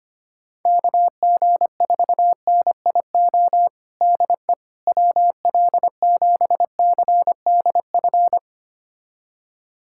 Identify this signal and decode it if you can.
Morse code, heard as a keyed tone, about 25 wpm, 700 Hz. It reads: KG4NIO DE WL7CDF
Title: USA Callsign DE Callsign - 6 Characters - 25wpm